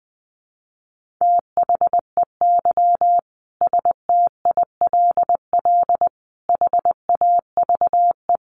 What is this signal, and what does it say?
Morse code, heard as a keyed tone, about 20 wpm, 700 Hz. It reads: THEY STILL HAVE
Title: Sentences from Top 1000 Words 20wpm